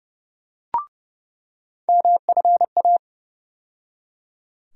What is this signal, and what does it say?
Morse code, heard as a keyed tone, about 32 wpm, 700 Hz. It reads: MFA
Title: USA Call Sign Postfixes - 3 Letters 30wpm